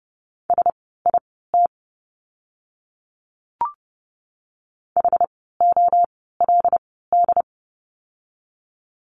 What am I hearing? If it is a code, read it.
Morse code, heard as a keyed tone, about 30 wpm, spaced as for 17 wpm, 700 Hz. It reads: SIT HOLD
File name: Lesson 16f - Letter c - Words 30wpm